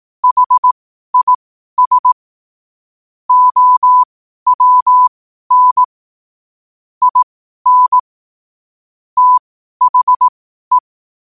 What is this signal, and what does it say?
Morse code, heard as a keyed tone, about 18 wpm, spaced as for 12 wpm, 1000 Hz. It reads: HIS OWN IN THE
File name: qotdfiles / 12wpm